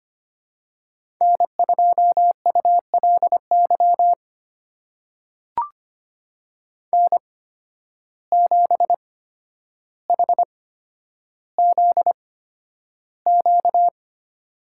Morse code, heard as a keyed tone, about 25 wpm, spaced as for 11 wpm, 700 Hz. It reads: N2ULY N7HZQ
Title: USA Call Signs - 5 Characters - 8x Character Spacing - 25wpm